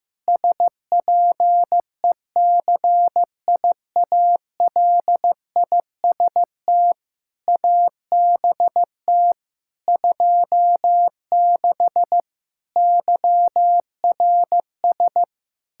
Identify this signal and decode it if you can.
Morse code, heard as a keyed tone, about 15 wpm, 700 Hz. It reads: SPECIALIST ABT 26 YRS
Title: Practice QSO - Exchange 3 - More Info - 15wpm